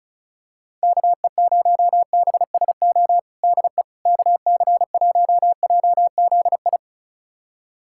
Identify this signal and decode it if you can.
Morse code, heard as a keyed tone, about 36 wpm, 700 Hz. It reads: KE0BSO DE KC1JZI
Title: USA Callsign DE Callsign - 6 Characters - 35wpm